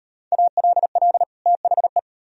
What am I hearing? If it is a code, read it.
Morse code, heard as a keyed tone, about 36 wpm, 700 Hz. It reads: ALL THE